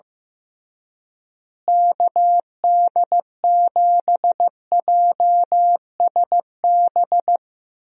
Morse code, heard as a keyed tone, about 15 wpm, 700 Hz. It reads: KD7JSB
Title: USA Call Signs - 6 Characters - Repeated 3x - 15wpm